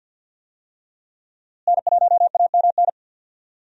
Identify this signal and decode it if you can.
Morse code, heard as a keyed tone, about 53 wpm, 700 Hz. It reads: N1AMN
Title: USA Call Signs - 5 Characters - Repeated 3x - 50wpm